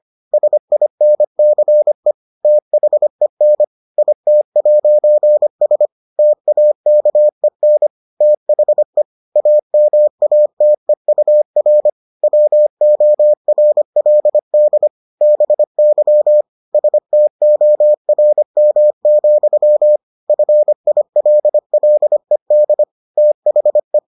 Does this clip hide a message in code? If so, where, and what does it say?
Yes: Morse code from 0.3 to 24.0 s, heard as a keyed tone, about 25 wpm, 600 Hz. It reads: SINCE THEN IT'S TAKEN THE AMATEUR WORLD BY STORM, FILLED THE